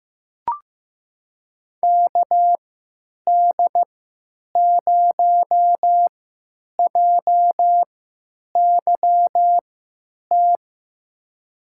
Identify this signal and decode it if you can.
Morse code, heard as a keyed tone, about 15 wpm, spaced as for 9 wpm, 700 Hz. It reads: KD0JYT
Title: USA Call Signs - 6 Characters - 3x Character Spacing - 15wpm